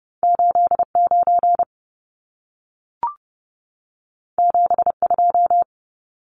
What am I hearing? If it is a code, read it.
Morse code, heard as a keyed tone, about 30 wpm, 700 Hz. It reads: 89 72
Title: Numbers - 2-Digit 30wpm